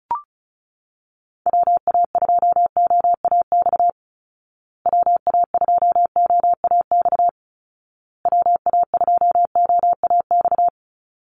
Morse code, heard as a keyed tone, about 34 wpm, 700 Hz. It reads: WA2OAX WA2OAX WA2OAX
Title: Call Signs Repeated 2x - 35wpm